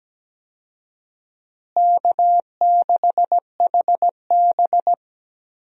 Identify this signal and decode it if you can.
Morse code, heard as a keyed tone, about 17 wpm, 700 Hz. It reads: K6HB
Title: USA Call Signs - 4 Characters - Repeated 2x - 17wpm